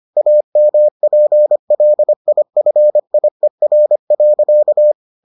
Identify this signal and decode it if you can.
Morse code, heard as a keyed tone, about 25 wpm, 600 Hz. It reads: AMPLIFIER.